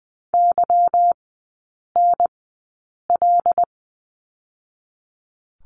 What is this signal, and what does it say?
Morse code, heard as a keyed tone, about 20 wpm, spaced as for 8 wpm, 700 Hz. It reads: YNL
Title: 184 Lesson 30e - Letter b - Three Characters 20wpm